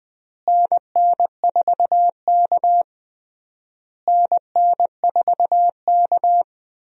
Morse code, heard as a keyed tone, about 20 wpm, 700 Hz. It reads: NN4K NN4K
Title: USA Call Signs - 4 Characters - Repeated 3x - 20wpm